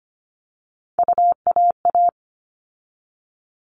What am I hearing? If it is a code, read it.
Morse code, heard as a keyed tone, about 25 wpm, 700 Hz. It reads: UAA
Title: All Cut Numbers - 3-Digit 25wpm